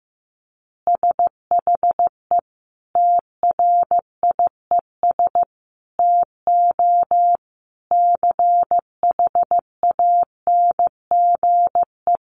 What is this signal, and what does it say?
Morse code, heard as a keyed tone, about 14 wpm, 700 Hz. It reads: SHE TRIES TO CHANGE
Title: Easy Phrases 15wpm